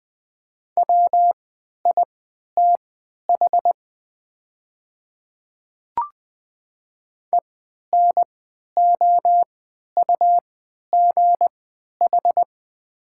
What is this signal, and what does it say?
Morse code, heard as a keyed tone, about 20 wpm, spaced as for 11 wpm, 700 Hz. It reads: WITH ENOUGH
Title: 3x Character Spacing - Top 500 Words - 20wpm